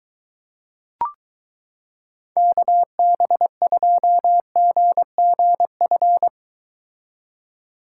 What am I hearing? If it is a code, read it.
Morse code, heard as a keyed tone, about 22 wpm, 700 Hz. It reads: KB2GGF